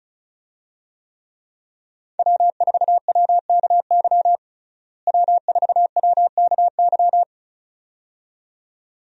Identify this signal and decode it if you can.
Morse code, heard as a keyed tone, about 36 wpm, 700 Hz. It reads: W4WKY W4WKY
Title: USA Call Signs - 5 Characters - Repeated 2x - 35wpm